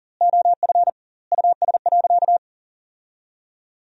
Morse code, heard as a keyed tone, about 37 wpm, 700 Hz. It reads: OF US.
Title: 2x Word-Spacing - Sentences from Top 500 Words - 40wpm